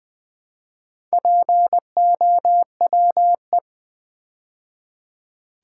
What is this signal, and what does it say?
Morse code, heard as a keyed tone, about 20 wpm, 700 Hz. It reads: POWE
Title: Build A Word - 2 Letters At A Time - 8-Letter Words - 20wpm